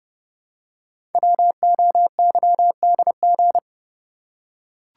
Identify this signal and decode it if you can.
Morse code, heard as a keyed tone, about 30 wpm, 700 Hz. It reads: WOYDG
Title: Code Groups - 5 Letters - 30wpm